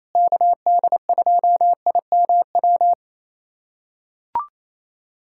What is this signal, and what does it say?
Morse code, heard as a keyed tone, about 28 wpm, 700 Hz. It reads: KD2IMW
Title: USA Callsign DE Callsign 28wpm